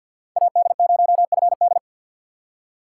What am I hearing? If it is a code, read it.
Morse code, heard as a keyed tone, about 47 wpm, 700 Hz. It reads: AD0FD
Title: USA Call Signs - 5 Characters - 6x Character Spacing - 50wpm